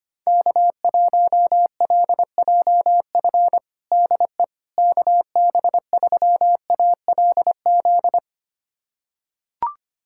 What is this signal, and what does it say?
Morse code, heard as a keyed tone, about 25 wpm, 700 Hz. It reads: K1LJF DE KB3ALZ
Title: USA Callsign DE Callsign 25wpm